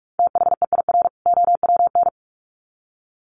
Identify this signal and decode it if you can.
Morse code, heard as a keyed tone, about 44 wpm, 700 Hz. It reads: THEIR OWN
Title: Sets of 2 Words 45wpm